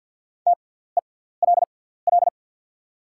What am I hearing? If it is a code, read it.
Morse code, heard as a keyed tone, about 48 wpm, spaced as for 17 wpm, 690 Hz. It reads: TELL
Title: 6x Character Spacing - Top 200 Words - 50wpm